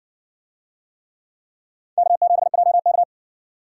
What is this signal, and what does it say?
Morse code, heard as a keyed tone, about 60 wpm, 700 Hz. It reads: K7JK